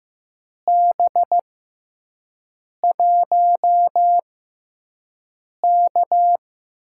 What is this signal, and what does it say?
Morse code, heard as a keyed tone, about 15 wpm, spaced as for 5 wpm, 700 Hz. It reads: B1K
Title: USA Call Signs - 4 Characters - 6x Character Spacing - 15wpm